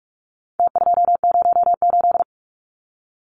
Morse code, heard as a keyed tone, about 44 wpm, 700 Hz. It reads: T208
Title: Numbers - Contest Style - 4-Digit 45wpm